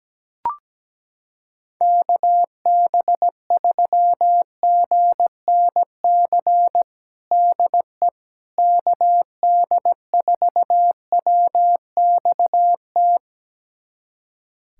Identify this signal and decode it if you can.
Morse code, heard as a keyed tone, about 17 wpm, 700 Hz. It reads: KB3GNC DE KD4WXT
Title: USA Callsign DE Callsign - 6 Characters - 17wpm